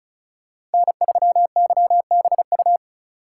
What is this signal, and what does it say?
Morse code, heard as a keyed tone, about 34 wpm, 700 Hz. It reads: N3YBU
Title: USA Call Signs - 5 Characters - 4x Character Spacing - 35wpm